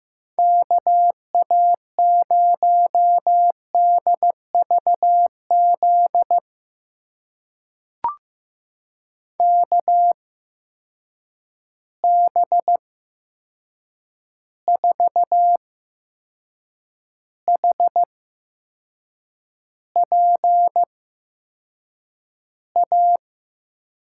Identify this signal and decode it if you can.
Morse code, heard as a keyed tone, about 15 wpm, spaced as for 6 wpm, 700 Hz. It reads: KA0DVZ KB4HPA
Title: USA Call Signs - 6 Characters - 8x Character Spacing - 15wpm